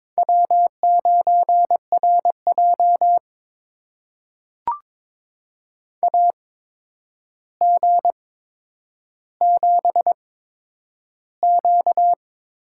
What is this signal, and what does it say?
Morse code, heard as a keyed tone, about 22 wpm, spaced as for 9 wpm, 700 Hz. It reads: W9RJ AG7Q
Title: USA Call Signs - 4 Characters - 8x Character Spacing - 22wpm